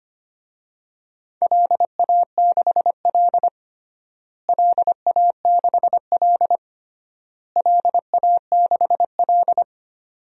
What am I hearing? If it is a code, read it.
Morse code, heard as a keyed tone, about 25 wpm, 700 Hz. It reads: LA6L LA6L LA6L
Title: DXCC-100 Call Signs - 4 Characters - Repeated 3x - 25wpm